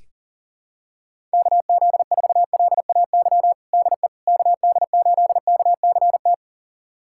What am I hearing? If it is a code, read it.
Morse code, heard as a keyed tone, about 40 wpm, 700 Hz. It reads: KZ4LAY DE KD8KCT
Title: USA Callsign DE Callsign - 6 Characters - 40wpm